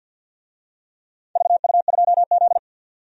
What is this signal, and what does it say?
Morse code, heard as a keyed tone, about 51 wpm, 700 Hz. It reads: VU2Z
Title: DXCC-100 Call Signs - 4 Characters - 50wpm